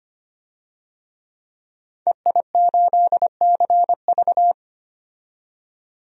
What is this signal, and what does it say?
Morse code, heard as a keyed tone, about 25 wpm, 700 Hz. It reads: EI8CV
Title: DXCC-100 Call Signs - 5 Characters - 25wpm